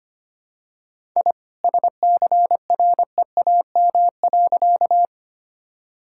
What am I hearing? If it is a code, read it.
Morse code, heard as a keyed tone, about 25 wpm, 700 Hz. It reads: I SCREAM.
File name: practice / 25wpm / All-Age Appropriate Jokes - 25wpm